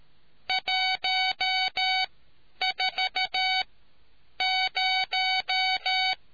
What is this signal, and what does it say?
Morse code, heard as a keyed tone, about 13 wpm, spaced as for 8 wpm, 750 Hz. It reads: 140